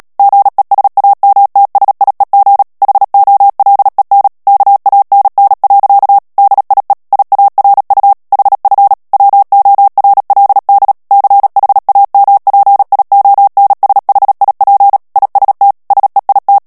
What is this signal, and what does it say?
Morse code, heard as a keyed tone, about 37 wpm, 800 Hz. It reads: GESAMTSIEG HOLEN KANN. DIE IARU HF WORLD CHAMPIONSHIP IST SEIT
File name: morse mp3 / ALLEMAND texte clair 2 vitesse 35 mots minute